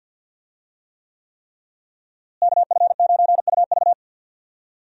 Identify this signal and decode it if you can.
Morse code, heard as a keyed tone, about 52 wpm, 700 Hz. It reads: KF9UV
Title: USA Call Signs - 5 Characters - Repeated 3x - 50wpm